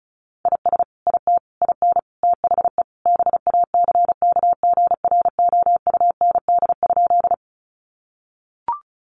Morse code, heard as a keyed tone, about 34 wpm, 700 Hz. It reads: IS IT IN THE BACKGROUND?
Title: Sentences from Top 1300 Words 35wpm